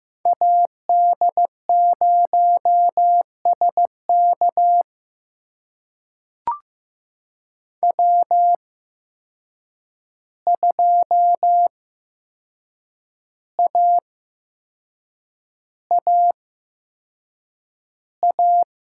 Morse code, heard as a keyed tone, about 15 wpm, spaced as for 6 wpm, 700 Hz. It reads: AD0SK W2AAA
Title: USA Call Signs - 5 Characters - 8x Character Spacing - 15wpm